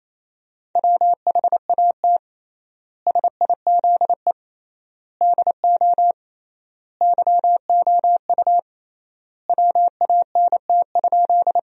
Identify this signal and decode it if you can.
Morse code, heard as a keyed tone, about 28 wpm, 700 Hz. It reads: WHAT SIZE DO YOU WANT?
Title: 3x Word-Spacing - Sentences from Top 400 Words - 28wpm